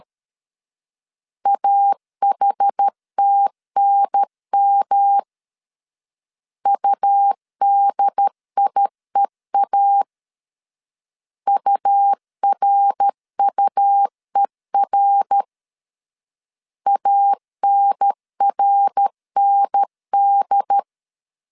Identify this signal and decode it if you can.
Morse code, heard as a keyed tone, about 13 wpm, 800 Hz. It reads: AHTNM UDIEA URUER ANRND
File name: CW / DET / 1 3 1500C S C 12wpm 2750